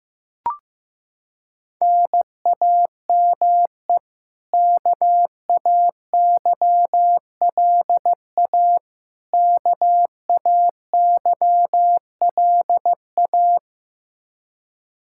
Morse code, heard as a keyed tone, about 15 wpm, 700 Hz. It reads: NAME KAYLA KAYLA
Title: Practice QSO - Exchange 1 - Beginning - 15wpm